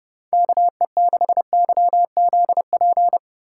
Morse code, heard as a keyed tone, about 31 wpm, 700 Hz. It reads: KE6YZP
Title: Speed Racing - USA 2x3 Call Signs 25wpm